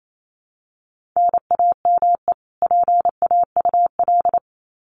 Morse code, heard as a keyed tone, about 28 wpm, 700 Hz. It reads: NAME PAUL
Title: Common QSO Phrases 28wpm